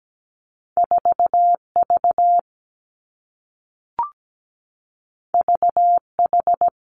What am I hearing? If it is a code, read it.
Morse code, heard as a keyed tone, about 17 wpm, 700 Hz. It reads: 4V VH
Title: Two Challenging Letter-Number Combinations 17wpm